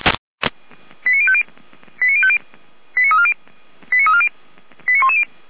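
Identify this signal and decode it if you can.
Five-tone selective call (ZVEI), heard as quick runs of five tones, tones of about 70 ms. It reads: 89050 89050 89340 89340 89110
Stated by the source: ZVEI five-tone call received by SDR.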